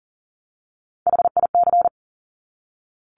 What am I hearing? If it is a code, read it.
Morse code, heard as a keyed tone, about 39 wpm, 700 Hz. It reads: HIC
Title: Top Three Letter Combinations in Words 40wpm